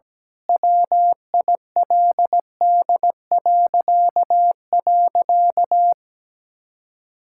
Morse code, heard as a keyed tone, about 17 wpm, 700 Hz. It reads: WILD..
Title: Pangram Challenge v2 - 17wpm